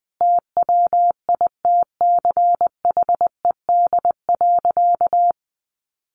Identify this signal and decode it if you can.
Morse code, heard as a keyed tone, about 19 wpm, 700 Hz. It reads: TWITCHED.
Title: A Sound Of Thunder 20wpm